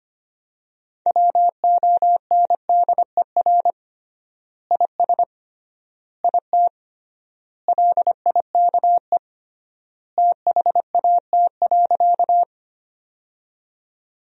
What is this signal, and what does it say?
Morse code, heard as a keyed tone, about 25 wpm, 700 Hz. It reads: WONDER IS IT LIKE THAT.
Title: 3x Word-Spacing - Sentences from Top 500 Words - 25wpm